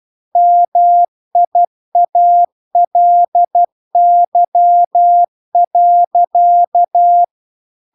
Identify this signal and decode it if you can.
Morse code, heard as a keyed tone, about 12 wpm, 700 Hz. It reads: MIALY.
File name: recordings / 12wpm0000